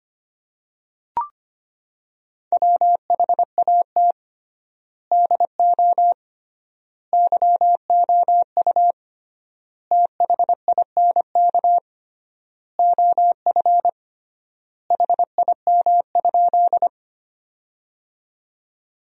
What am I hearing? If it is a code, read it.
Morse code, heard as a keyed tone, about 25 wpm, 700 Hz. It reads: WHAT DO YOU THINK OF HIM?
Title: 3x Word-Spacing - Sentences from Top 200 Words - 25wpm